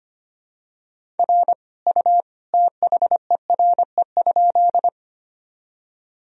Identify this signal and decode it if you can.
Morse code, heard as a keyed tone, about 25 wpm, 700 Hz. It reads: R U THERE?
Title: Sentences from Top 100 Words and Abbreviations 25wpm